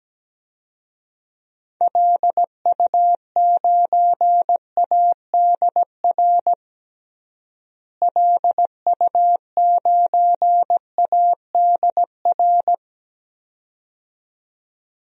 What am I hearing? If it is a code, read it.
Morse code, heard as a keyed tone, about 17 wpm, 700 Hz. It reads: LU9ADR LU9ADR
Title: DXCC-100 Call Signs - 6 Characters - Repeated 2x - 17wpm